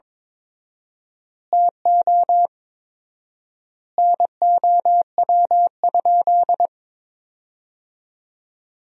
Morse code, heard as a keyed tone, about 22 wpm, 700 Hz. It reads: TO NOW?
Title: 4x Word-Spacing - Sentences from Top 200 Words - 22wpm